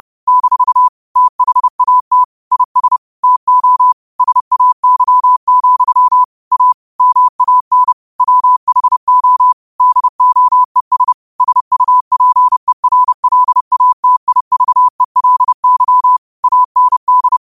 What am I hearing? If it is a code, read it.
Morse code, heard as a keyed tone, about 30 wpm, 1000 Hz. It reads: - THAT IS TO SAY, A MAN WHO DOES SUPERLATIVELY AND
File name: qotdfiles / 30wpm